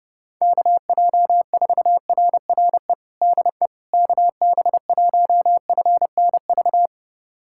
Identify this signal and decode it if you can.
Morse code, heard as a keyed tone, about 29 wpm, 700 Hz. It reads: KJ4RRE DE KB1FNV
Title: USA Callsign DE Callsign - 6 Characters - 30wpm